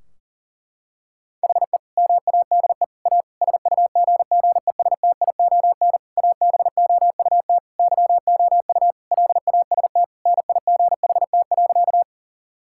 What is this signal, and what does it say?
Morse code, heard as a keyed tone, about 37 wpm, 700 Hz. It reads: HE MADE A SUGGESTION ABOUT YOU LAST NIGHT.
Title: Sentences from Top 1600 Words 40wpm